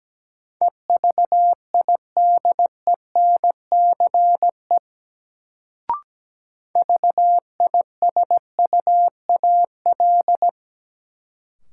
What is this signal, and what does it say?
Morse code, heard as a keyed tone, about 17 wpm, 700 Hz. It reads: EVIDENCE VISUAL
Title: Sets of 5 Words Encore - New Word Drill - 17wpm